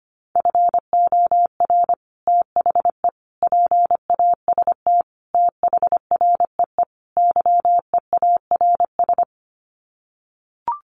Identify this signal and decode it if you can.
Morse code, heard as a keyed tone, about 25 wpm, 700 Hz. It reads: FOR THE PAST THREE YEARS
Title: Sets of 5 Words Encore2 25wpm